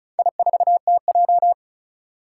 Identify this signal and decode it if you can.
Morse code, heard as a keyed tone, about 36 wpm, 690 Hz. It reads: I4TJ
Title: Four Challenging Letter-Number Combinations 35wpm